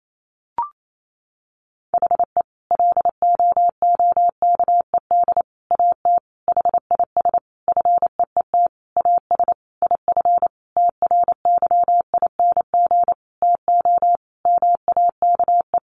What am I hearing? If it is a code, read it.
Morse code, heard as a keyed tone, about 28 wpm, 700 Hz. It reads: HE LOOKED AT HIS FEET AS IF TRYING TO MAKE
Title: A Sound Of Thunder 28wpm